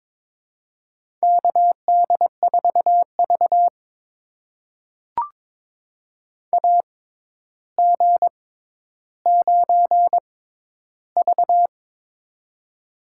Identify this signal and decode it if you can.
Morse code, heard as a keyed tone, about 22 wpm, spaced as for 11 wpm, 700 Hz. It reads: KD4V AG9V
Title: USA Call Signs - 4 Characters - 6x Character Spacing - 22wpm